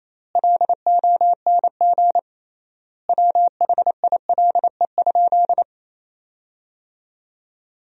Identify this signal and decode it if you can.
Morse code, heard as a keyed tone, about 28 wpm, 700 Hz. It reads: LONG WHILE?
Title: 3x Word-Spacing - Sentences from Top 300 Words - 28wpm